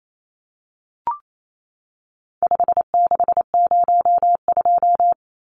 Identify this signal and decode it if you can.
Morse code, heard as a keyed tone, about 28 wpm, 700 Hz. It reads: 5602
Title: Numbers - 4-Digit 28wpm